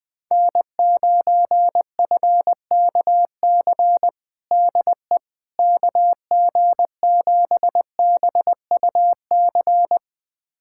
Morse code, heard as a keyed tone, about 20 wpm, 700 Hz. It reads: N9FKC DE KG7BUC
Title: USA Callsign DE Callsign 20wpm